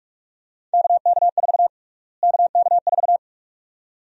Morse code, heard as a keyed tone, about 46 wpm, 700 Hz. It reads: KK4 KK4
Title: USA Call Sign Prefixes - Repeated 2x - 45wpm